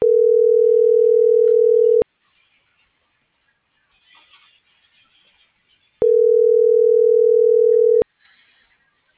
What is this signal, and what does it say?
Ringback tone (North America), 440 + 480 Hz.